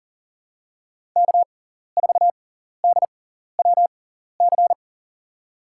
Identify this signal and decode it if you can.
Morse code, heard as a keyed tone, about 40 wpm, spaced as for 14 wpm, 700 Hz. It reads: K4DWC
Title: USA Call Signs - 6x Character Spacing - 40wpm